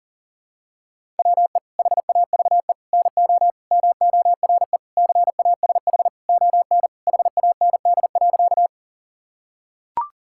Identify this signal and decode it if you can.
Morse code, heard as a keyed tone, about 37 wpm, 690 Hz. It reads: WE HAVE NO MORE CASH ON HAND.
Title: Sentences from Top 1000 Words - Review Part 5 - 40wpm